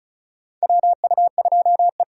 Morse code, heard as a keyed tone, about 36 wpm, 700 Hz. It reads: WU2E
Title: USA Call Signs - 4 Characters - 4x Character Spacing - 35wpm